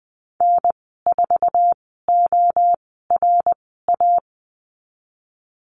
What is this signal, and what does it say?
Morse code, heard as a keyed tone, about 20 wpm, spaced as for 14 wpm, 700 Hz. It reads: N4ORA
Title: 082 Lesson 14g - Number 2 - Call Signs 20wpm